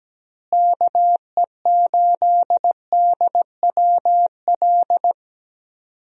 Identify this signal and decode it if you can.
Morse code, heard as a keyed tone, about 17 wpm, 700 Hz. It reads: KE8DWL
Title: USA Call Signs - 6 Characters - Repeated 3x - 17wpm